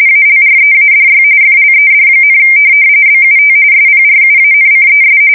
RTTY, about 45.45 baud. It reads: ALREADY FLED, BUT SPRUANCES
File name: fsk170 / USS-New-Jersey